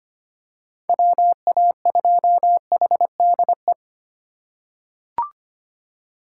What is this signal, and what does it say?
Morse code, heard as a keyed tone, about 25 wpm, 700 Hz. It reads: WA2HDE
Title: USA Call Signs - 6 Characters - 2x Character Spacing - 25wpm